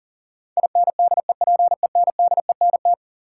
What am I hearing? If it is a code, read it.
Morse code, heard as a keyed tone, about 38 wpm, 700 Hz. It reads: INDEPENDENT
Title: Sentences from Top 900 Words - New Word Drill - 40wpm